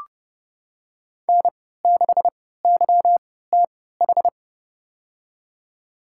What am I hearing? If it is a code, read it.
Morse code, heard as a keyed tone, about 31 wpm, spaced as for 17 wpm, 700 Hz. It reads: N6YTH
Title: USA Call Signs - 3x Character Spacing - 30wpm